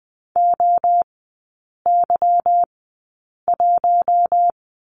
Morse code, heard as a keyed tone, about 20 wpm, spaced as for 8 wpm, 700 Hz. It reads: OY1
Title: 159 Lesson 26e - Letter g - Three Characters 20wpm